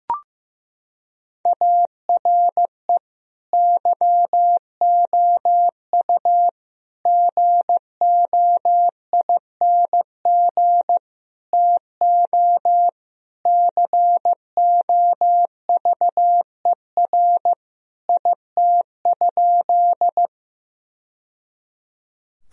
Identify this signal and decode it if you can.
Morse code, heard as a keyed tone, about 15 wpm, 700 Hz. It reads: ARE YOU GOING TO COVER IT?
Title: Sentences from Top 1000 Words - Review Part 1 - 15wpm